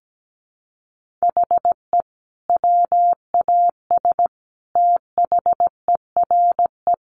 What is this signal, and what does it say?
Morse code, heard as a keyed tone, about 17 wpm, 700 Hz. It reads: HE WAS THERE
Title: Sentences from Top 200 Words 17wpm